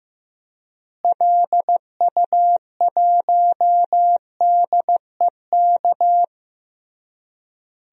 Morse code, heard as a keyed tone, about 15 wpm, 700 Hz. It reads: LU1DEK
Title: DXCC-100 Call Signs - 6 Characters - Repeated 3x - 15wpm